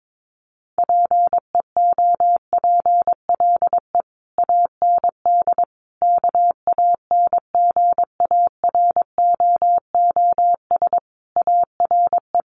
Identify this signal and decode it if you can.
Morse code, heard as a keyed tone, about 22 wpm, 700 Hz. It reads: PEOPLE AND KANGAROOS ARE
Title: Easy Phrases 22wpm